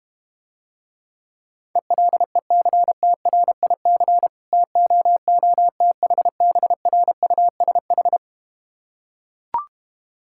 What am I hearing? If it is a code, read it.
Morse code, heard as a keyed tone, about 31 wpm, 700 Hz. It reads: ELECTRIC TOOTHBRUSH